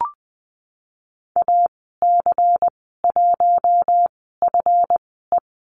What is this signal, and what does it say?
Morse code, heard as a keyed tone, about 20 wpm, spaced as for 15 wpm, 700 Hz. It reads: AC1FE
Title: 154 Lesson 25g - Letter p - Call Signs 20wpm